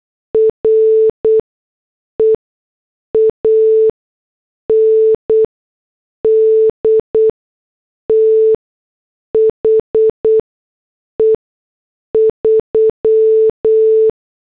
Morse code, heard as a keyed tone, about 8 wpm, spaced as for 6 wpm, 440 Hz. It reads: RE AND THE 3
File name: morse / headlines